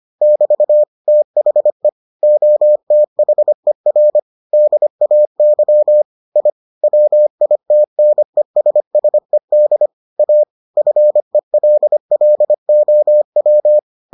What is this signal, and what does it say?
Morse code, heard as a keyed tone, about 25 wpm, 600 Hz. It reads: <BT> THE OTHER DAY I WITNESSED A FELLOW